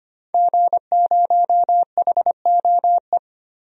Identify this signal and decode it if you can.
Morse code, heard as a keyed tone, about 25 wpm, 700 Hz. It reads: G0HOE
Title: DXCC-100 Call Signs - 5 Characters - Repeated 3x - 25wpm